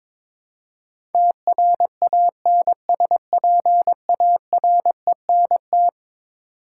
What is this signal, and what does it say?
Morse code, heard as a keyed tone, about 22 wpm, 700 Hz. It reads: TRANSPARENT
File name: practice / 22wpm / Eleven Letter Words - 22wpm